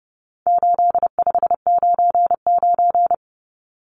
Morse code, heard as a keyed tone, about 30 wpm, 700 Hz. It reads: 8599
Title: Numbers - 4-Digit 30wpm